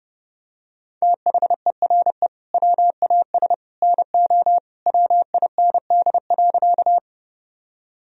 Morse code, heard as a keyed tone, about 30 wpm, 700 Hz. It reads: THERE WAS NO WIND.
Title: Sentences from Top 1000 Words - Review Part 2 - 30wpm